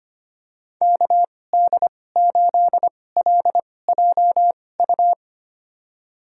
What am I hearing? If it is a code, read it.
Morse code, heard as a keyed tone, about 25 wpm, spaced as for 18 wpm, 700 Hz. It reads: KD8LJU
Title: USA Call Signs - 6 Characters - 2x Character Spacing - 25wpm